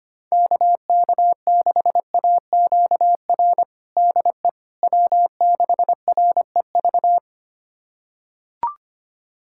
Morse code, heard as a keyed tone, about 25 wpm, 700 Hz. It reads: KK6AQR DE W6REV
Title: USA Callsign DE Callsign 25wpm